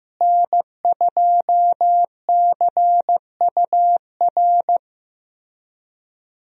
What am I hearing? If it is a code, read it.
Morse code, heard as a keyed tone, about 15 wpm, 700 Hz. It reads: N2CUR
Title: USA Call Signs - 5 Characters - Repeated 3x - 15wpm